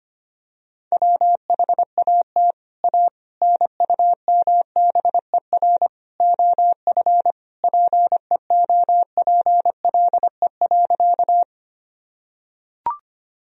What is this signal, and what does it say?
Morse code, heard as a keyed tone, about 25 wpm, 700 Hz. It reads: WHAT A NUMBER OF PEOPLE.
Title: Sentences from Top 1000 Words - Review Part 1 - 25wpm